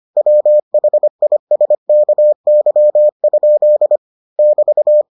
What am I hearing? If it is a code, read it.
Morse code, heard as a keyed tone, about 25 wpm, 600 Hz. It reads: WHISKY? <BT>